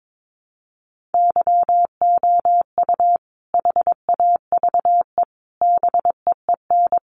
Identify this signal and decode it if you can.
Morse code, heard as a keyed tone, about 22 wpm, 700 Hz. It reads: YOU HAVE BEEN
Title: Sentences from Top 700 Words 22wpm